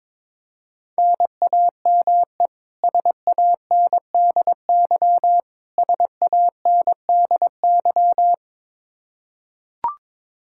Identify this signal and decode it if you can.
Morse code, heard as a keyed tone, about 22 wpm, 700 Hz. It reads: NAME SANDY SANDY
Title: K1USN SST QSO Practice 22wpm